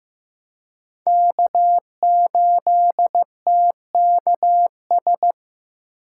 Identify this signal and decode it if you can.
Morse code, heard as a keyed tone, about 15 wpm, 700 Hz. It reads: K8TKS
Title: USA Call Signs - 5 Characters - 3x Character Spacing - 15wpm